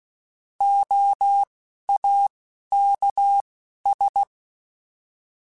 Morse code, heard as a keyed tone, about 16 wpm, spaced as for 11 wpm, 800 Hz. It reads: OAKS